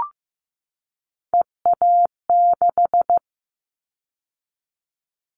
Morse code, heard as a keyed tone, about 15 wpm, 700 Hz. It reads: EA6
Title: Callsign Prefixes for Top 100 DXCC - 15wpm